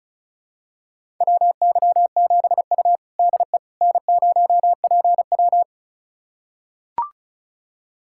Morse code, heard as a keyed tone, about 36 wpm, 700 Hz. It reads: WY7U DE N0PW
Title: USA Callsign DE Callsign - 4 Characters - 35wpm